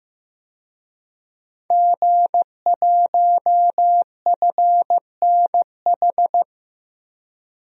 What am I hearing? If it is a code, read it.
Morse code, heard as a keyed tone, about 15 wpm, 700 Hz. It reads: G1FNH
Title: DXCC-100 Call Signs - 5 Characters - 15wpm